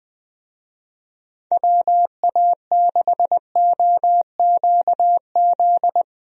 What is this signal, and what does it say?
Morse code, heard as a keyed tone, about 20 wpm, 700 Hz. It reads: WA6OQZ
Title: USA Call Signs - 6 Characters - 20wpm